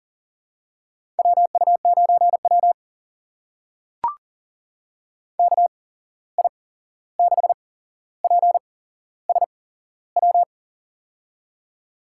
Morse code, heard as a keyed tone, about 40 wpm, spaced as for 15 wpm, 700 Hz. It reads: WU9W KI6PSW
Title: USA Call Signs - 8x Character Spacing - 40wpm